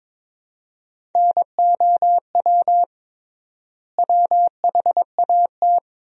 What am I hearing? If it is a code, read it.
Morse code, heard as a keyed tone, about 22 wpm, 700 Hz. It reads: NOW WHAT
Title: 3x Word-Spacing - Sentences from Top 100 Words - 22wpm